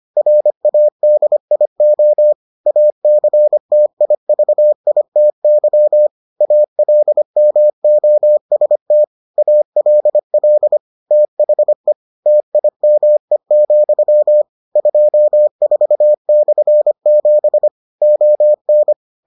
Morse code, heard as a keyed tone, about 25 wpm, 600 Hz. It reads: RADIO ACTIVITY ALMOST ALL THE TIME, 24/7 ON